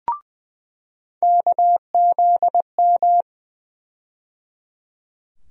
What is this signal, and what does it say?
Morse code, heard as a keyed tone, about 20 wpm, 700 Hz. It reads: KZM
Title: USA Call Sign Postfixes - 3 Letters 20wpm